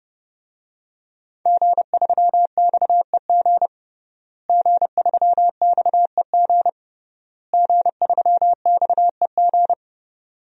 Morse code, heard as a keyed tone, about 30 wpm, 700 Hz. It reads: G3XEG G3XEG G3XEG
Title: DXCC-100 Call Signs - 5 Characters - Repeated 3x - 30wpm